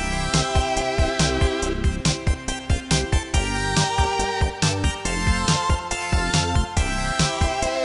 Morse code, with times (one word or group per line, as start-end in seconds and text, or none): none